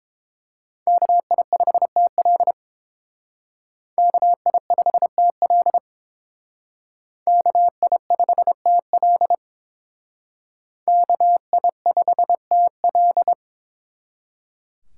Morse code, heard as a keyed tone, about 28 wpm, 700 Hz. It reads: KI5TL KI5TL KI5TL KI5TL